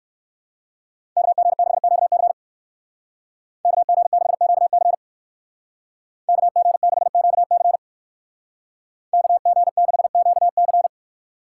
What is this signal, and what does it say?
Morse code, heard as a keyed tone, about 59 wpm, 700 Hz. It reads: KG6QC KG6QC KG6QC KG6QC